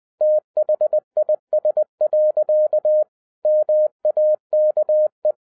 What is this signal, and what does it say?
Morse code, heard as a keyed tone, about 20 wpm, 610 Hz. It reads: THIS. MAKE
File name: 2011-10-mp3 / n13-TIPS FROM MY 34 YEARS OF EXPERIENCE 20wpm